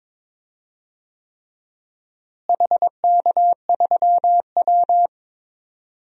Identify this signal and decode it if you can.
Morse code, heard as a keyed tone, about 22 wpm, 700 Hz. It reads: HK3W
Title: DXCC-100 Call Signs - 4 Characters - Repeated 3x - 22wpm